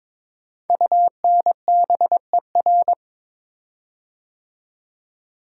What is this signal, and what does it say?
Morse code, heard as a keyed tone, about 22 wpm, 700 Hz. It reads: UNBER
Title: Code Groups - 5 Letters - 22wpm